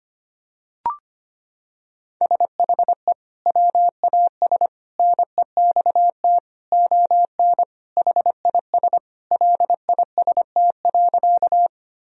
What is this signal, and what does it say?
Morse code, heard as a keyed tone, about 25 wpm, 700 Hz. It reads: SHE WAS NEXT ON HIS LIST.
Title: Sentences from Top 1000 Words - Review Part 2 - 25wpm